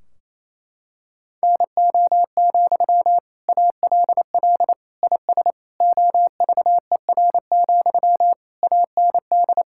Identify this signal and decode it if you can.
Morse code, heard as a keyed tone, about 28 wpm, 700 Hz. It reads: NO, ALL IS OVER, AND
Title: Sentences from Top 1600 Words 28wpm